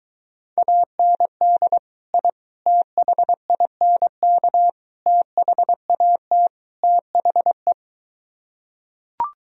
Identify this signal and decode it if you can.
Morse code, heard as a keyed tone, about 23 wpm, 700 Hz. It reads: AND I THINK THAT THE